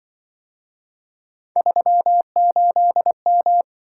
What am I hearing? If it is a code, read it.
Morse code, heard as a keyed tone, about 24 wpm, 700 Hz. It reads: 38M
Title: Speed-Racing - Three Letter-Number Combinations - 20wpm